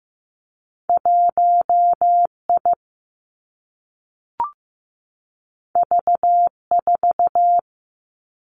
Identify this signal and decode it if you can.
Morse code, heard as a keyed tone, about 15 wpm, 700 Hz. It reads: 1 I V 4
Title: Two Challenging Letter-Number Combinations 15wpm